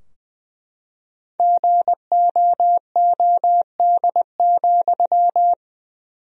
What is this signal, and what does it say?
Morse code, heard as a keyed tone, about 20 wpm, 700 Hz. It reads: GOOD,
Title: 4x Word-Spacing - Sentences from Top 200 Words - 20wpm